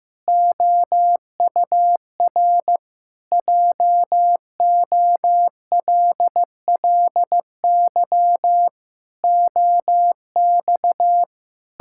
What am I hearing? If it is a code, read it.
Morse code, heard as a keyed tone, about 15 wpm, 700 Hz. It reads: OUR JOLLY OX
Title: Pangram Challenge v2 - 15wpm